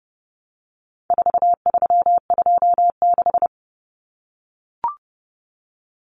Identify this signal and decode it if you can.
Morse code, heard as a keyed tone, about 30 wpm, 700 Hz. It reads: 4326
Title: Common Cut Numbers - 4-Digit 30wpm